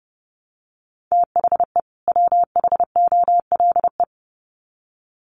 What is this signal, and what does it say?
Morse code, heard as a keyed tone, about 30 wpm, 700 Hz. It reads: THE WHOLE
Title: Sets of 2 Words 30wpm